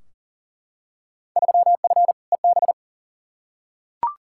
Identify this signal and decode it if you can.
Morse code, heard as a keyed tone, about 37 wpm, 700 Hz. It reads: 3F EB
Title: Field Day Exchanges - 40wpm